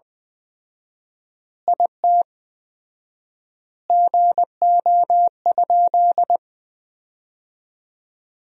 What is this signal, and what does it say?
Morse code, heard as a keyed tone, about 20 wpm, 700 Hz. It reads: IT GO?
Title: 4x Word-Spacing - Sentences from Top 200 Words - 20wpm